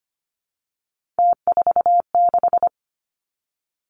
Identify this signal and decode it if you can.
Morse code, heard as a keyed tone, about 25 wpm, 700 Hz. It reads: T46
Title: Three Challenging Letter-Number Combinations 25wpm